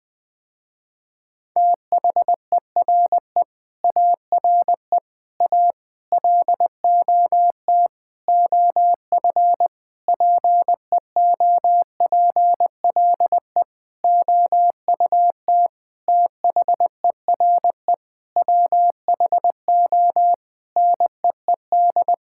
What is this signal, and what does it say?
Morse code, heard as a keyed tone, about 20 wpm, 700 Hz. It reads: THERE ARE A LOT OF PEOPLE OUT THERE WHO NEED
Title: Sentences with 8-Word Phrases - Spoken Beginning - 20wpm